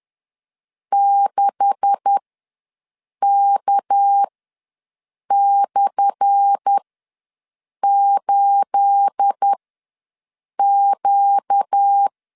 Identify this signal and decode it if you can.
Morse code, heard as a keyed tone, about 10 wpm, spaced as for 6 wpm, 800 Hz. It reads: 6K/8Q
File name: LEZIONI / 10 SIGNS / 10B INT ws 10wpm 4134